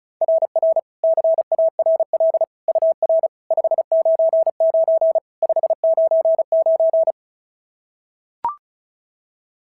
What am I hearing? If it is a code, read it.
Morse code, heard as a keyed tone, about 36 wpm, 690 Hz. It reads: RR CARL UR 599 599
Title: SOTA QSO Practice 35wpm